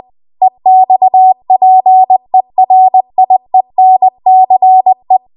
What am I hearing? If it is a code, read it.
Morse code, heard as a keyed tone, about 20 wpm, 750 Hz. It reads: EXPERIENCE
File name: Archive / 20 WPM / 20WPM